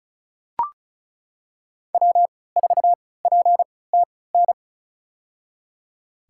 Morse code, heard as a keyed tone, about 34 wpm, spaced as for 20 wpm, 700 Hz. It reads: W4PTN
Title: USA Call Signs - 5 Characters - 3x Character Spacing - 35wpm